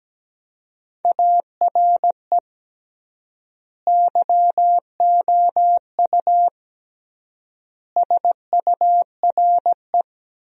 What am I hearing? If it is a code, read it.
Morse code, heard as a keyed tone, about 17 wpm, 700 Hz. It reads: ARE YOU SURE
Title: 3x Word-Spacing - Sentences from Top 300 Words - 17wpm